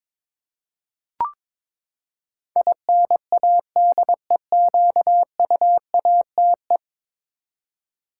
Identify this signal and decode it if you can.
Morse code, heard as a keyed tone, about 22 wpm, 700 Hz. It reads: INADEQUATE
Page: morse-code-ninja-practice-22wpm